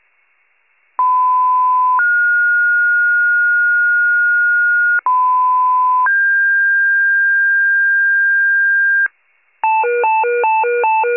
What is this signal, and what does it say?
Two-tone page, heard as a one-second tone, then a three-second tone, about 1010 then 1430 Hz, about 1010 then 1600 Hz, with noise. It is heard from 1.0 to 9.1 s.
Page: 94 RESCUE DUTY